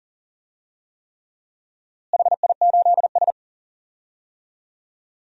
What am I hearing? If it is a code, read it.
Morse code, heard as a keyed tone, about 40 wpm, 700 Hz. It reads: HI8S
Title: DXCC-100 Call Signs - 4 Characters - Repeated 3x - 40wpm